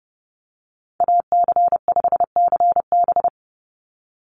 Morse code, heard as a keyed tone, about 30 wpm, 700 Hz. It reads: AC5CB
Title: USA Call Signs 30wpm